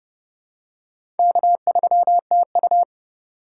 Morse code, heard as a keyed tone, about 32 wpm, 700 Hz. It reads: K3TU
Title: USA Call Signs - 4 Characters - 2x Character Spacing - 30wpm